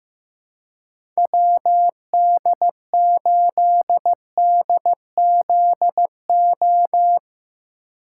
Morse code, heard as a keyed tone, about 16 wpm, 700 Hz. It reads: WD8DZO
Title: USA Call Signs - 6 Characters - 3x Character Spacing - 15wpm